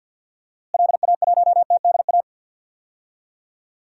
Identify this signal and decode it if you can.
Morse code, heard as a keyed tone, about 53 wpm, 700 Hz. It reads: LA1TDA
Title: DXCC-100 Call Signs - 6 Characters - Repeated 2x - 50wpm